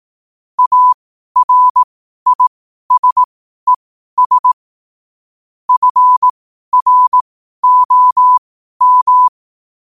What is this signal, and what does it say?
Morse code, heard as a keyed tone, about 18 wpm, spaced as for 12 wpm, 1000 Hz. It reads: ARISES FROM
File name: qotdfiles / 12wpm